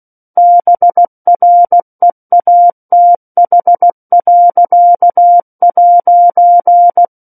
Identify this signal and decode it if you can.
Morse code, heard as a keyed tone, about 16 wpm, 700 Hz. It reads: BREATH.'
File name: 03-Chapter-03 16 WPM